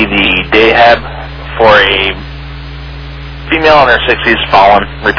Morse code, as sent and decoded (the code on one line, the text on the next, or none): none